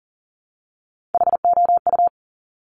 Morse code, heard as a keyed tone, about 40 wpm, 700 Hz. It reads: HOU
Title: Top Three Letter Combinations in Words 40wpm